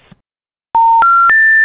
Special information tone.